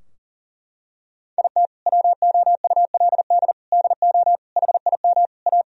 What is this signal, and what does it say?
Morse code, heard as a keyed tone, about 40 wpm, 700 Hz. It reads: IT WOULD DO HIM A
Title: Sentences from Top 1000 Words - Review Part 3 - 40wpm